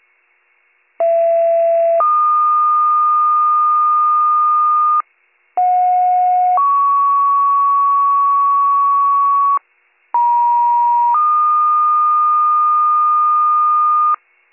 Two-tone page, heard as a one-second tone, then a three-second tone, about 670 then 1220 Hz, about 730 then 1120 Hz, about 950 then 1220 Hz. It is heard from 1.0 to 14.2 s.